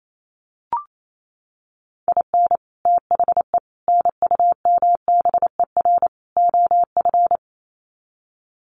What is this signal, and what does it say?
Morse code, heard as a keyed tone, about 28 wpm, 700 Hz. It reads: IN THE NUMBER OF
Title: Sets of 4 Words 28wpm